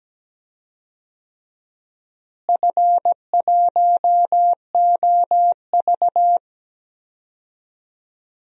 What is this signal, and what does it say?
Morse code, heard as a keyed tone, about 17 wpm, 700 Hz. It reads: F1OV
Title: DXCC-100 Call Signs - 4 Characters - Repeated 3x - 17wpm